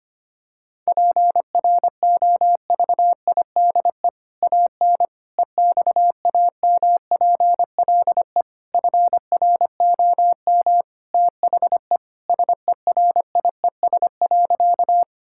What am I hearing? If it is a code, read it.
Morse code, heard as a keyed tone, about 25 wpm, 700 Hz. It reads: PROVIDE AN EXAMPLE FROM THE SERIES.
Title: Sentences from Top 1000 Words - Review Part 5 - 25wpm